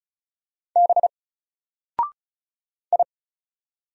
Morse code, heard as a keyed tone, about 33 wpm, 700 Hz. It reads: B I
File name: Single-Letter-Number-35wpm